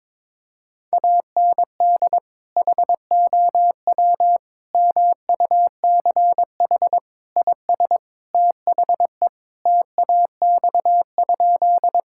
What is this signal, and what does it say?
Morse code, heard as a keyed tone, about 22 wpm, 700 Hz. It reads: AND HOW MUCH IS THE TAX?
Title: Sentences from Top 1000 Words - Review Part 5 - 22wpm